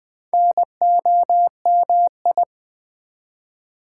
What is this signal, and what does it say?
Morse code, heard as a keyed tone, about 21 wpm, 700 Hz. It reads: NOMI
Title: Build A Word - 2 Letters At A Time - 8-Letter Words - 20wpm